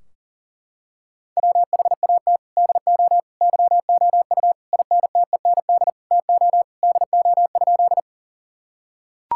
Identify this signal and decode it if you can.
Morse code, heard as a keyed tone, about 40 wpm, 700 Hz. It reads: WHAT DO YOU INTEND TO DO?
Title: Sentences from Top 1600 Words 40wpm